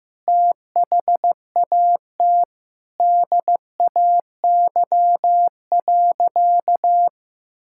Morse code, heard as a keyed tone, about 15 wpm, 700 Hz. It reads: THAT DAY.
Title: Sentences from Top 1000 Words - Review Part 5 - 15wpm